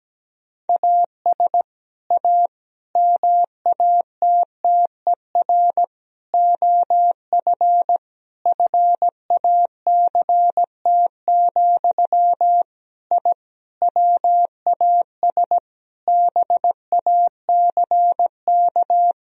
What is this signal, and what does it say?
Morse code, heard as a keyed tone, about 17 wpm, 700 Hz. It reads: AS A MATTER OF FACT, I WAS BACK
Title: Sentences from Top 1000 Words - Review Part 4 - 17wpm